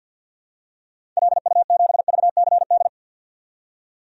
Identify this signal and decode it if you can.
Morse code, heard as a keyed tone, about 52 wpm, 700 Hz. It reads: LU7VCD
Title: DXCC-100 Call Signs - 6 Characters - 50wpm